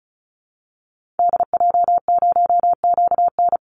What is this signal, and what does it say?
Morse code, heard as a keyed tone, about 34 wpm, 700 Hz. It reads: DJ0QN
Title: Call Signs Repeated 2x - 35wpm